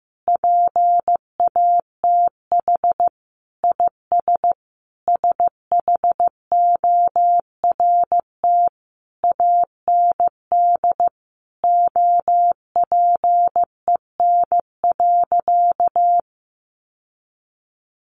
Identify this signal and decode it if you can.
Morse code, heard as a keyed tone, about 15 wpm, 700 Hz. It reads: PATH IS SHORT AND OPEN.
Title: Sentences from Top 1400 Words 15wpm